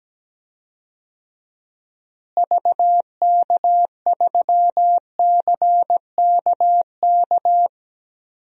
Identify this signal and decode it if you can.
Morse code, heard as a keyed tone, about 17 wpm, 700 Hz. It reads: VK3CKK
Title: DXCC-100 Call Signs - 6 Characters - Repeated 3x - 17wpm